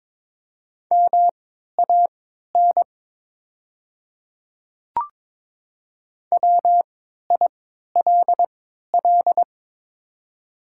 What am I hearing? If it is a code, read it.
Morse code, heard as a keyed tone, about 22 wpm, spaced as for 13 wpm, 690 Hz. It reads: MAN WILL
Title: 3x Character Spacing - Top 300 Words - 22wpm